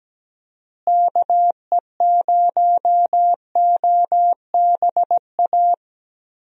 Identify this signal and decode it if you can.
Morse code, heard as a keyed tone, about 17 wpm, 700 Hz. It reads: KE0OBA
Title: USA Call Signs - 6 Characters - Repeated 2x - 17wpm